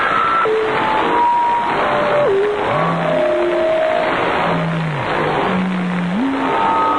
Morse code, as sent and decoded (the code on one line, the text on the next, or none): none